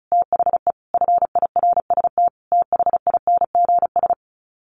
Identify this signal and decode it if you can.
Morse code, heard as a keyed tone, about 34 wpm, 700 Hz. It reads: THE FIRST THINGS
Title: Sets of 5 Words Encore 35wpm